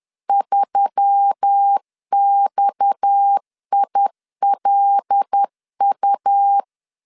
Morse code, heard as a keyed tone, about 10 wpm, 800 Hz. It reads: 3XILU
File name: LEZIONI / 7 GF9Z / 7 GF 10wpm 2514